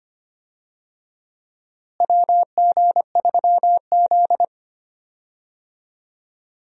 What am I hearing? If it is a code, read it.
Morse code, heard as a keyed tone, about 25 wpm, 700 Hz. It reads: WG3Z
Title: USA Call Signs - 4 Characters - Repeated 2x - 25wpm